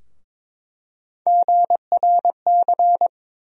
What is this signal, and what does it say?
Morse code, heard as a keyed tone, about 22 wpm, 700 Hz. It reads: GRC
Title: USA Call Sign Postfixes - 3 Letters 22wpm